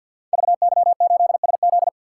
Morse code, heard as a keyed tone, about 48 wpm, 700 Hz. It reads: VY8SZ
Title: Code Groups - 5 Letter:Numbers - 50wpm